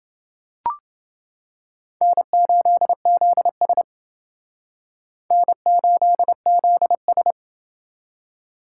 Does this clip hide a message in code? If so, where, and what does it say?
Yes: Morse code from 2.0 to 7.3 s, heard as a keyed tone, about 29 wpm, 700 Hz. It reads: N8ZS N8ZS